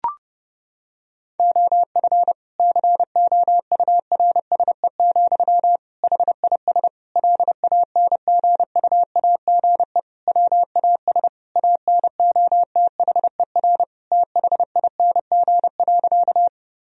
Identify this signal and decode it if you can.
Morse code, heard as a keyed tone, about 30 wpm, 700 Hz. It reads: OF COURSE, HIS LANGUAGE WAS ANOTHER THING.
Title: Sentences from Top 1000 Words - Review Part 3 - 30wpm